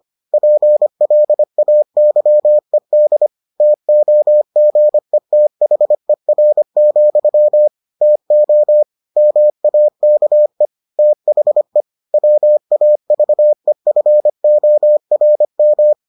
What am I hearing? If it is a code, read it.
Morse code, heard as a keyed tone, about 25 wpm, 600 Hz. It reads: PLAYED TOGETHER, TO MAKE THE WAVEFORM